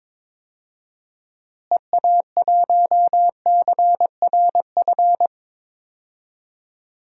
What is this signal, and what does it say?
Morse code, heard as a keyed tone, about 22 wpm, 700 Hz. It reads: EA1CRF
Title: DXCC-100 Call Signs - 6 Characters - Repeated 2x - 22wpm